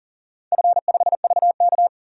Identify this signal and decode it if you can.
Morse code, heard as a keyed tone, about 42 wpm, 700 Hz. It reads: F5VK
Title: DXCC-100 Call Signs - 4 Characters - 40wpm